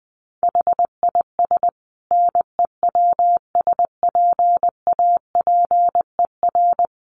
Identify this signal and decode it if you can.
Morse code, heard as a keyed tone, about 20 wpm, 700 Hz. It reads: HIS NEWSPAPER
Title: Sentences from Top 1100 Words 20wpm